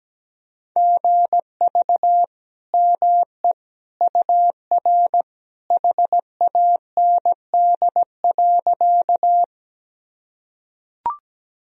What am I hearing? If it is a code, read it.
Morse code, heard as a keyed tone, about 17 wpm, 700 Hz. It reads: GV ME UR HAND.
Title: Sentences from Top 200 Words and Abbreviations 17wpm